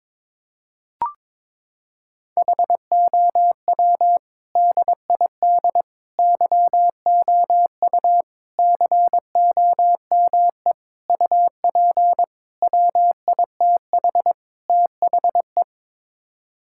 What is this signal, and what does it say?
Morse code, heard as a keyed tone, about 21 wpm, 700 Hz. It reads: HOW DID YOU COME UP WITH THE
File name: practice / 22wpm / Sets of 7 Words - 22wpm